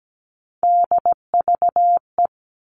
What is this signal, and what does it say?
Morse code, heard as a keyed tone, about 17 wpm, 700 Hz. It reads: DVE
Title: All Cut Numbers - 3-Digit 17wpm